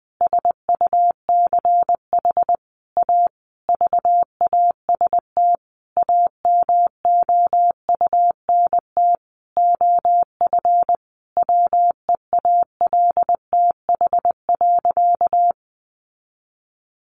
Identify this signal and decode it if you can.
Morse code, heard as a keyed tone, about 20 wpm, 700 Hz. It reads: SUCH A VAST AMOUNT OF WEALTH.
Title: Sentences from Top 1300 Words 20wpm